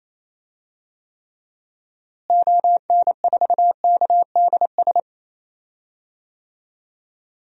Morse code, heard as a keyed tone, about 27 wpm, 700 Hz. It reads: ON4KDS
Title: DXCC-100 Call Signs - 6 Characters - Repeated 3x - 28wpm